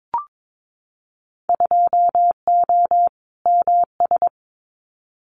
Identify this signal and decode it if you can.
Morse code, heard as a keyed tone, about 22 wpm, 700 Hz. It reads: 2O MS
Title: Winter Field Day Exchanges 22wpm